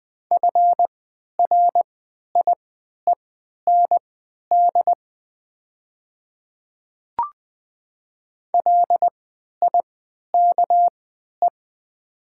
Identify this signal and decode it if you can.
Morse code, heard as a keyed tone, about 20 wpm, spaced as for 11 wpm, 690 Hz. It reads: FRIEND LIKE
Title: 3x Character Spacing - Top 300 Words - 20wpm